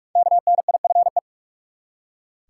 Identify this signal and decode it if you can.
Morse code, heard as a keyed tone, about 42 wpm, 700 Hz. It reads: KNIFE
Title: Five Letter Words - 45wpm